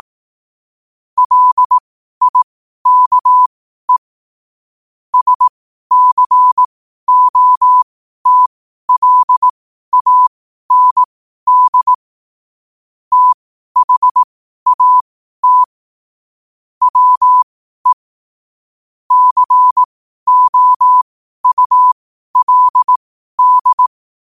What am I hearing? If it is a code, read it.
Morse code, heard as a keyed tone, about 18 wpm, spaced as for 12 wpm, 1000 Hz. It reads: LIKE SCOTLAND THAT WE COULD
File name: qotdfiles / 12wpm